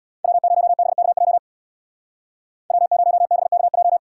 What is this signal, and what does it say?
Morse code, heard as a keyed tone, about 64 wpm, 700 Hz. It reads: W0BKJ W0BKJ